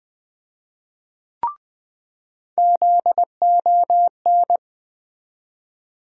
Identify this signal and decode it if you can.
Morse code, heard as a keyed tone, about 20 wpm, 700 Hz. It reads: ZON
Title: USA Call Sign Postfixes - 3 Letters 20wpm